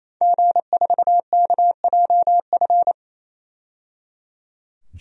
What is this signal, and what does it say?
Morse code, heard as a keyed tone, about 28 wpm, 700 Hz. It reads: G4KJF
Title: DXCC-100 Call Signs - 5 Characters - 28wpm